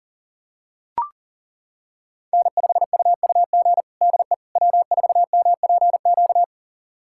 Morse code, heard as a keyed tone, about 40 wpm, 700 Hz. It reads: N5UUG DE W4MPQ
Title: USA Callsign DE Callsign - 5 Characters - 40wpm